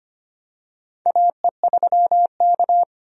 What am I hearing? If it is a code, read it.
Morse code, heard as a keyed tone, about 25 wpm, 700 Hz. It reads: AE3K
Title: USA Call Signs - 4 Characters - 25wpm